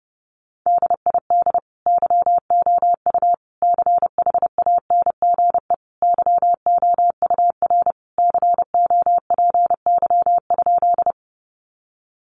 Morse code, heard as a keyed tone, about 30 wpm, 700 Hz. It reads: DID YOU CHANGE YOUR COPY?
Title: Sentences from Top 1400 Words 30wpm